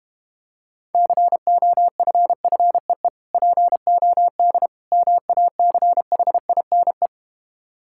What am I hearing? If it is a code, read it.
Morse code, heard as a keyed tone, about 32 wpm, 700 Hz. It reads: COFFEE POD MACHINE